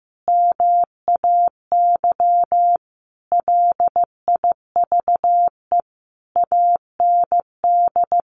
Morse code, heard as a keyed tone, about 15 wpm, 700 Hz. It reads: MAY LIVE AND
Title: Sentences from Top 800 Words 15wpm